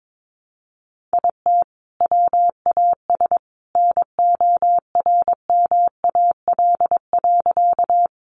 Morse code, heard as a keyed tone, about 22 wpm, 700 Hz. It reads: IT WAS NORMAL.
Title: Sentences from Top 1200 Words 22wpm